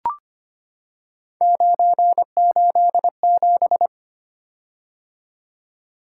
Morse code, heard as a keyed tone, about 25 wpm, 700 Hz. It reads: 987
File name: Three Letter-Number Combinations - 25wpm